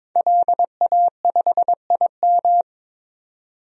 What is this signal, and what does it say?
Morse code, heard as a keyed tone, about 23 wpm, 700 Hz. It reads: LA5IM
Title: DXCC-100 Call Signs - 5 Characters - 22wpm